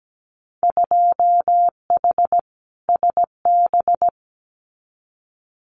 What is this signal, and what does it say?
Morse code, heard as a keyed tone, about 17 wpm, 700 Hz. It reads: 2H SB
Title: Winter Field Day Exchanges Encore 17wpm